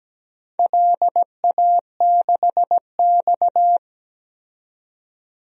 Morse code, heard as a keyed tone, about 17 wpm, 700 Hz. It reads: LA6X
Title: DXCC-100 Call Signs - 4 Characters - 17wpm